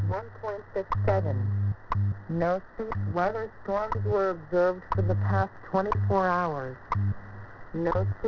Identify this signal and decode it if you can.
Radio time signal, with speech.
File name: WebSDR records / WWV 5000.00 am